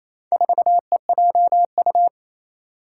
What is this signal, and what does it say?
Morse code, heard as a keyed tone, about 28 wpm, 700 Hz. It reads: 4EJU
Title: Four Challenging Letter-Number Combinations 28wpm